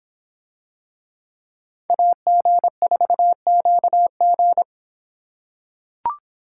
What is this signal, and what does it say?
Morse code, heard as a keyed tone, about 26 wpm, 700 Hz. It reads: AG4QG